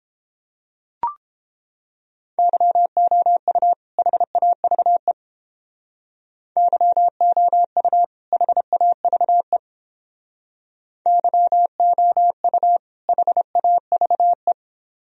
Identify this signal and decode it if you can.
Morse code, heard as a keyed tone, about 29 wpm, 700 Hz. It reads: YOU HAVE YOU HAVE YOU HAVE